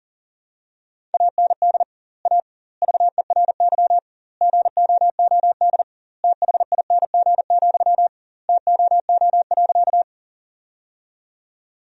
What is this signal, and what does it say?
Morse code, heard as a keyed tone, about 40 wpm, 700 Hz. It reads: AND A VERY GOOD THING, TOO.
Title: 2x Word-Spacing - Sentences from Top 200 Words - 40wpm